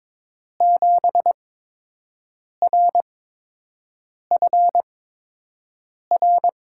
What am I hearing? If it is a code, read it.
Morse code, heard as a keyed tone, about 22 wpm, spaced as for 6 wpm, 700 Hz. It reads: 7RFR
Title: USA Call Signs - 8x Character Spacing - 22wpm